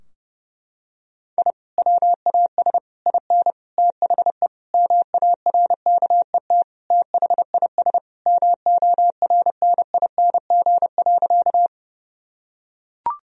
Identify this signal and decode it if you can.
Morse code, heard as a keyed tone, about 30 wpm, 700 Hz. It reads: I WAS IN THE MARKET THIS MORNING.
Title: Sentences from Top 1000 Words - Review Part 3 - 30wpm